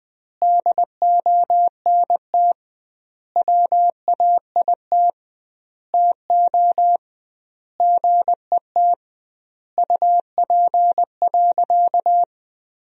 Morse code, heard as a keyed tone, about 20 wpm, 700 Hz. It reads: DONT WAIT TO GET UP.
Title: 2x Word-Spacing - Sentences from Top 500 Words - 20wpm